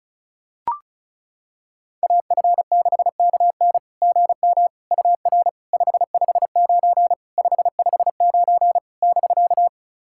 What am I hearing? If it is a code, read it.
Morse code, heard as a keyed tone, about 34 wpm, 690 Hz. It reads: AF6KN GM UR 559 559 <BK>
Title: SOTA QSO Practice 35wpm